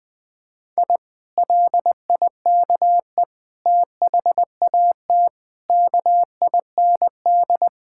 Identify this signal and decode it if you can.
Morse code, heard as a keyed tone, about 21 wpm, 700 Hz. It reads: I LIKE THAT KIND
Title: Sentences with 3-Word Phrases - 20wpm